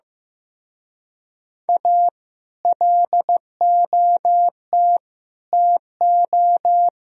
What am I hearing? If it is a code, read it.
Morse code, heard as a keyed tone, about 15 wpm, 700 Hz. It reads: A LOT TO
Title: Speed-Racing - Sets of 5 Words 15wpm